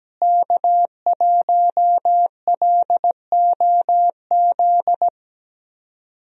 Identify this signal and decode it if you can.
Morse code, heard as a keyed tone, about 17 wpm, 700 Hz. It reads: K1LOZ
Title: USA Call Signs - 5 Characters - Repeated 3x - 17wpm